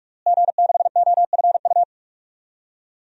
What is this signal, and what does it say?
Morse code, heard as a keyed tone, about 42 wpm, 700 Hz. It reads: G6OFU
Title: DXCC-100 Call Signs - 5 Characters - Repeated 2x - 45wpm